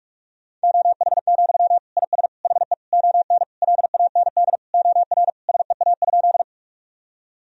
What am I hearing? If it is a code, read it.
Morse code, heard as a keyed tone, about 46 wpm, 700 Hz. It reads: OH, IS HE ON LAND OR SEA?
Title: Sentences from Top 1000 Words - Review Part 1 - 45wpm